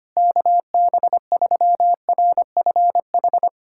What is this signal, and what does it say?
Morse code, heard as a keyed tone, about 25 wpm, 700 Hz. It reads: KB3RFH
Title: USA Callsign DE Callsign 25wpm